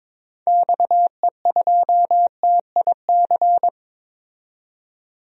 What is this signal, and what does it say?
Morse code, heard as a keyed tone, about 22 wpm, 700 Hz. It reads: XE2TIC
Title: DXCC-100 Call Signs - 6 Characters - 22wpm